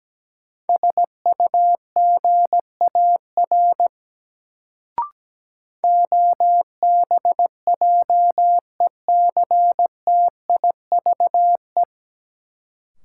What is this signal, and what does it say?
Morse code, heard as a keyed tone, about 17 wpm, 700 Hz. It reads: SUGAR OBJECTIVE
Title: Sentences from Top 1200 Words - New Word Drill - 17wpm